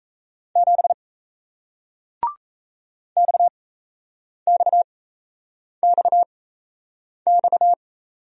Morse code, heard as a keyed tone, about 38 wpm, spaced as for 8 wpm, 700 Hz. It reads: 7 XXXX